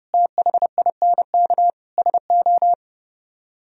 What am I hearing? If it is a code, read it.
Morse code, heard as a keyed tone, about 36 wpm, 700 Hz. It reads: THINK SO
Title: Speed-Racing - Sets of 4 Words 30wpm